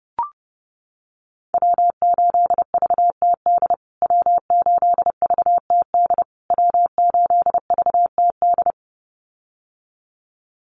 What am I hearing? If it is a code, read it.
Morse code, heard as a keyed tone, about 30 wpm, 700 Hz. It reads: W8VTD W8VTD W8VTD
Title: USA Call Signs Repeated 3x 30wpm